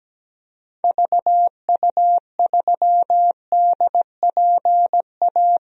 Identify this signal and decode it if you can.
Morse code, heard as a keyed tone, about 17 wpm, 700 Hz. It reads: VU3DPA
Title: DXCC-100 Call Signs - 6 Characters - 17wpm